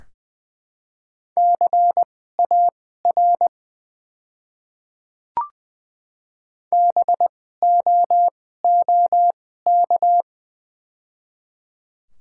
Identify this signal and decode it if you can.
Morse code, heard as a keyed tone, about 20 wpm, spaced as for 14 wpm, 700 Hz. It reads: CAR BOOK
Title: 2x Character Spacing - Top 500 Words - 20wpm